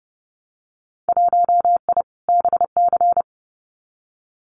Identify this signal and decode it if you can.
Morse code, heard as a keyed tone, about 30 wpm, 700 Hz. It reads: 1I BC
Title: Winter Field Day Exchanges Encore 30wpm